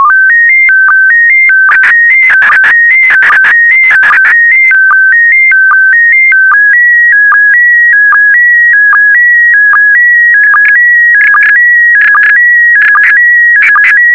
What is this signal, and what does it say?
SSTV (slow-scan television), PD 160.
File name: sstv / PD160-VK4MSL